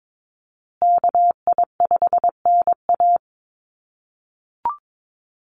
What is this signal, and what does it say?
Morse code, heard as a keyed tone, about 22 wpm, 700 Hz. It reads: KI5NA
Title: USA Call Signs Repeated 3x 22wpm